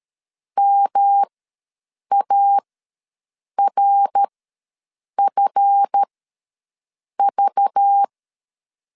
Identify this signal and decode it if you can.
Morse code, heard as a keyed tone, about 13 wpm, spaced as for 6 wpm, 800 Hz. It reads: MARFV
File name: LEZIONI / 8 YJ45 / 8 YJ4 ws 12wpm 3249